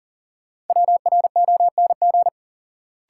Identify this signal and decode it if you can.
Morse code, heard as a keyed tone, about 40 wpm, 700 Hz. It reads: WRONG
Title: Sets of 5 Words Encore - New Word Drill - 40wpm